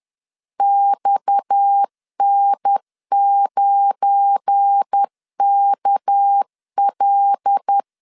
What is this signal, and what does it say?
Morse code, heard as a keyed tone, about 10 wpm, 800 Hz. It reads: XN9KL
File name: LEZIONI / 7 GF9Z / 7 GF9 10wpm 2747